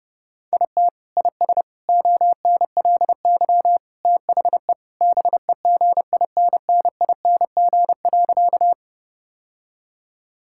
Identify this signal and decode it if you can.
Morse code, heard as a keyed tone, about 30 wpm, 700 Hz. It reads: IT IS ONLY THE BEGINNING.
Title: Sentences from Top 1000 Words - Review Part 5 - 30wpm